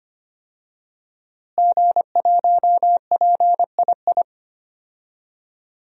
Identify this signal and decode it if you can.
Morse code, heard as a keyed tone, about 25 wpm, 700 Hz. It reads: G1PII
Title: DXCC-100 Call Signs - 5 Characters - Repeated 3x - 25wpm